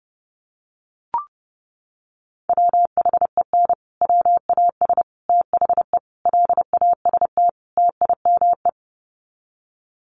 Morse code, heard as a keyed tone, about 30 wpm, 700 Hz. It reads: WHEN WAS THE LAST TIME
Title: Sets of 5 Words Encore 30wpm